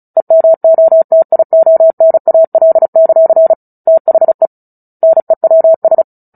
Morse code, heard as a keyed tone, about 34 wpm, 650 Hz. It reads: EMOTIONAL; THE NEWS
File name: Book Two / 09-Chapter 9 - 35 WPM